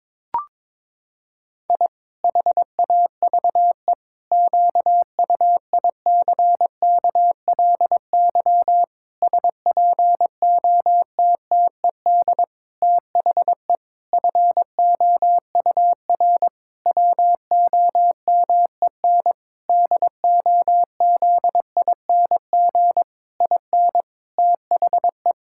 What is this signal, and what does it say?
Morse code, heard as a keyed tone, about 22 wpm, 700 Hz. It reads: I HAVE QUICKLY SPOTTED THE FOUR WOMEN DOZING IN THE
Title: Pangram Challenge v2 - 22wpm